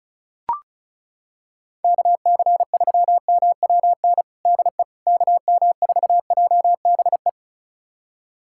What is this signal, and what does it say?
Morse code, heard as a keyed tone, about 34 wpm, 700 Hz. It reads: KC3MWN DE KM4JBE
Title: USA Callsign DE Callsign - 6 Characters - 35wpm